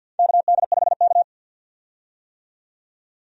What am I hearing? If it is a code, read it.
Morse code, heard as a keyed tone, about 50 wpm, 700 Hz. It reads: KD5K
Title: USA Call Signs - 4 Characters - 50wpm